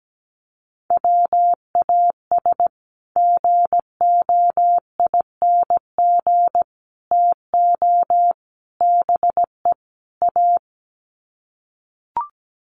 Morse code, heard as a keyed tone, about 17 wpm, 700 Hz. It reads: WAS GOING TO BE A
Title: Sets of 5 Words 17wpm